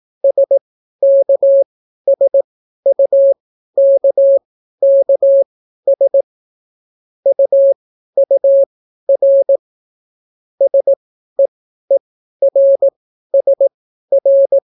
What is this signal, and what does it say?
Morse code, heard as a keyed tone, about 18 wpm, spaced as for 12 wpm, 550 Hz. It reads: SKSUKKS UUR SEERSR